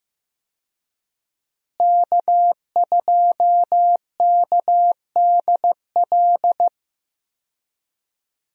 Morse code, heard as a keyed tone, about 15 wpm, 700 Hz. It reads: K2KDL
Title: USA Call Signs - 5 Characters - Repeated 2x - 15wpm